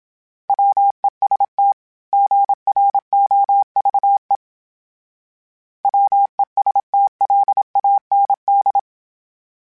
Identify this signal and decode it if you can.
Morse code, heard as a keyed tone, about 27 wpm, 800 Hz. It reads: WEST GROVE WESTLAND
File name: fromSMs / FromSMs mp3s / Cities@25